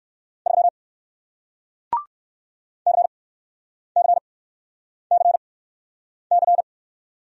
Morse code, heard as a keyed tone, about 60 wpm, spaced as for 9 wpm, 700 Hz. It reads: 3 CCCC